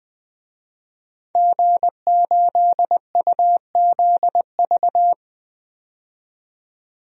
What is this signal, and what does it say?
Morse code, heard as a keyed tone, about 20 wpm, 700 Hz. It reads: G8UZV
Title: DXCC-100 Call Signs - 5 Characters - 20wpm